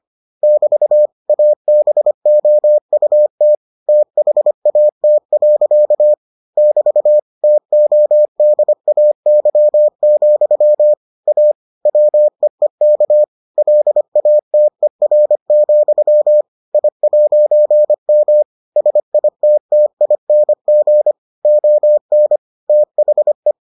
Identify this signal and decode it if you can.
Morse code, heard as a keyed tone, about 25 wpm, 600 Hz. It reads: <BT> ABOUT THAT. <BT> TODAY, A WEEK LATER, I'M SITTING ON THE